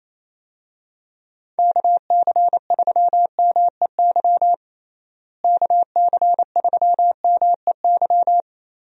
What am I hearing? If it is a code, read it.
Morse code, heard as a keyed tone, about 28 wpm, 700 Hz. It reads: KC3MEY KC3MEY
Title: USA Call Signs - 6 Characters - Repeated 3x - 28wpm